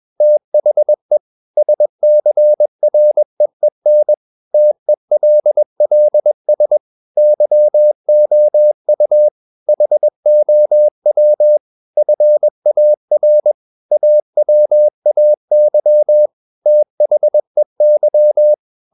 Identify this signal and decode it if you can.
Morse code, heard as a keyed tone, about 21 wpm, 600 Hz. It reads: THE SCREEN TELLS YOU HOW FAR AWAY THEY
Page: CW med Gnister